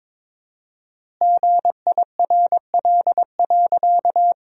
Morse code, heard as a keyed tone, about 22 wpm, 700 Hz. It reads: GIRL.
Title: 4x Word-Spacing - Sentences from Top 400 Words - 22wpm